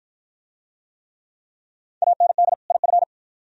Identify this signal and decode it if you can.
Morse code, heard as a keyed tone, about 51 wpm, 700 Hz. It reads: AND IF